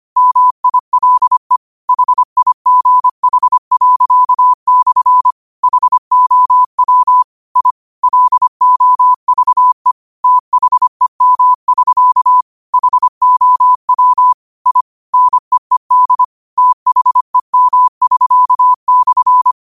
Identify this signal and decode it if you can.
Morse code, heard as a keyed tone, about 25 wpm, 1000 Hz. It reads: MILE HIGH./ HOW I LOVE THEM<SK> HOW I NEED THEM<SK>/
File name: qotdfiles / 25wpm